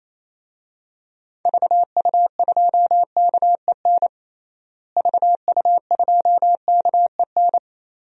Morse code, heard as a keyed tone, about 28 wpm, 700 Hz. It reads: VU2KEN VU2KEN
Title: DXCC-100 Call Signs - 6 Characters - Repeated 3x - 28wpm